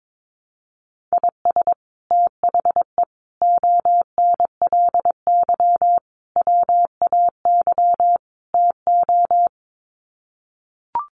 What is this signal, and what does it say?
Morse code, heard as a keyed tone, about 22 wpm, 700 Hz. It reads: IS THE ONLY WAY TO
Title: Sets of 5 Words Encore 22wpm